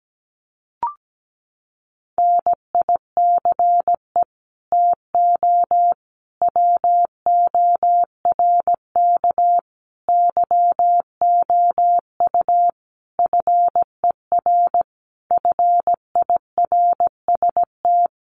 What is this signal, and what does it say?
Morse code, heard as a keyed tone, about 17 wpm, 700 Hz. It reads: NICE TO WORK YOU FER FIRST
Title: Common QSO Phrases 17wpm